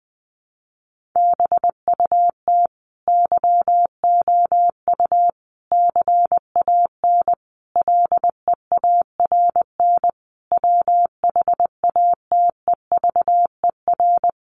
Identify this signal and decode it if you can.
Morse code, heard as a keyed tone, about 20 wpm, 700 Hz. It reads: BUT YOU CAN LEARN WHATEVER
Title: Sentences from Top 1500 Words 20wpm